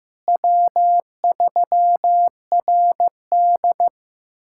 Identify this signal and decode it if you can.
Morse code, heard as a keyed tone, about 15 wpm, 700 Hz. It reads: W3RD
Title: USA Callsign DE Callsign - 4 Characters - 15wpm